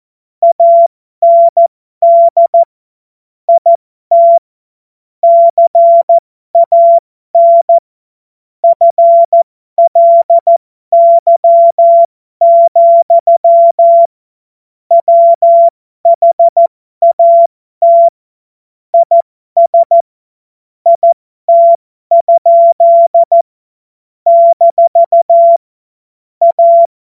Morse code, heard as a keyed tone, about 14 wpm, spaced as for 12 wpm, 680 Hz. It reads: AND IT CAN FLY, WHAT IS IT? - A
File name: Morsekurs / Witze